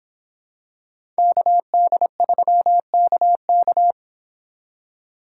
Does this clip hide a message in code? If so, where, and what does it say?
Yes: Morse code from 1.2 to 3.9 s, heard as a keyed tone, about 25 wpm, 700 Hz. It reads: KD3KK